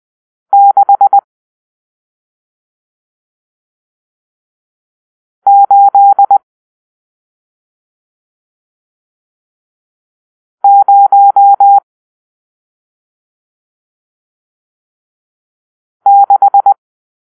Morse code, heard as a keyed tone, about 20 wpm, 800 Hz. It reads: 6 8 0 6
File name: arrl / Grp-7 Drill-1